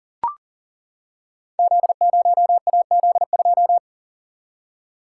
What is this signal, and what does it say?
Morse code, heard as a keyed tone, about 39 wpm, 700 Hz. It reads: Z0AZ2
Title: Code Groups - 5 Letter:Numbers - 40wpm